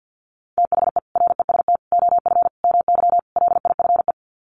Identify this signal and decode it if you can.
Morse code, heard as a keyed tone, about 50 wpm, 700 Hz. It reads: THE REST OF MY LIFE
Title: Sets of 5 Words 50wpm